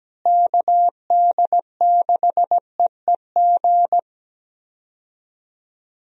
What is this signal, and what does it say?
Morse code, heard as a keyed tone, about 17 wpm, 700 Hz. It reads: KD6EEG
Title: USA Call Signs - 6 Characters - Repeated 2x - 17wpm